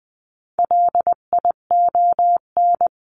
Morse code, heard as a keyed tone, about 20 wpm, 700 Hz. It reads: LION
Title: 259 Lesson 42d - Words 18-20wpm